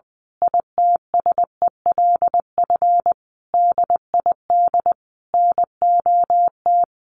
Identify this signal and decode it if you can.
Morse code, heard as a keyed tone, about 20 wpm, 700 Hz. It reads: ITSELF DID NOT
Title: A Sound Of Thunder 20wpm